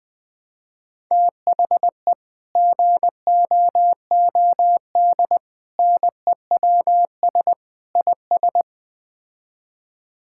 Morse code, heard as a keyed tone, about 20 wpm, 700 Hz. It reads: THE GOOD NEWS IS
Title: Build A Phrase - 5-Word Phrases - 20wpm